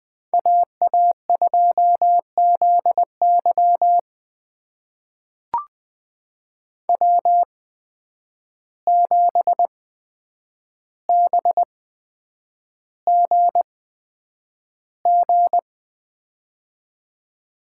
Morse code, heard as a keyed tone, about 20 wpm, spaced as for 9 wpm, 700 Hz. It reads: AA2ZY W7BGG
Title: USA Call Signs - 5 Characters - 8x Character Spacing - 20wpm